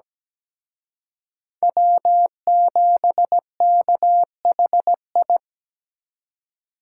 Morse code, heard as a keyed tone, about 20 wpm, 700 Hz. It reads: W7KHI
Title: Speed Racing - USA 1x3 Call Signs 17wpm